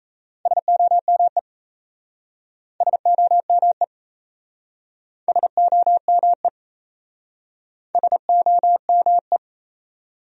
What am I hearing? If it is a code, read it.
Morse code, heard as a keyed tone, about 36 wpm, 700 Hz. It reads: SOME SOME SOME SOME